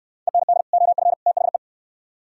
Morse code, heard as a keyed tone, about 66 wpm, 700 Hz. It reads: END OF THE